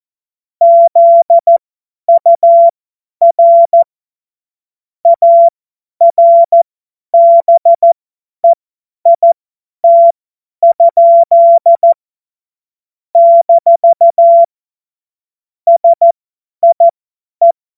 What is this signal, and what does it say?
Morse code, heard as a keyed tone, about 14 wpm, spaced as for 10 wpm, 680 Hz. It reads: ZUR ARBEIT? - SIE
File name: Morsekurs / Witze / Witz 16. Mai 2023 50 F70